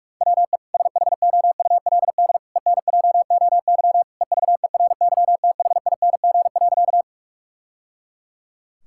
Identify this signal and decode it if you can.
Morse code, heard as a keyed tone, about 46 wpm, 700 Hz. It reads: WE SHOULD ENJOY EVERYTHING.
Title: Sentences from Top 1000 Words - Review Part 5 - 45wpm